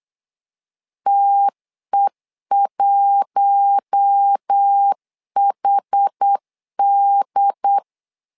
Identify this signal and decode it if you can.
Morse code, heard as a keyed tone, about 8 wpm, 800 Hz. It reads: TE1HD
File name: LEZIONI / 5 LBVC / 5 LB 8wpm 2816